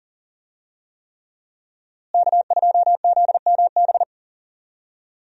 Morse code, heard as a keyed tone, about 40 wpm, 700 Hz. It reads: K2ZMB
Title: USA Call Signs - 5 Characters - Repeated 3x - 40wpm